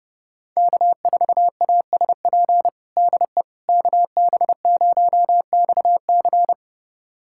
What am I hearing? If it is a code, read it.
Morse code, heard as a keyed tone, about 30 wpm, 700 Hz. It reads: K4ASP DE KB0XC
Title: USA Callsign DE Callsign - 5 Characters - 30wpm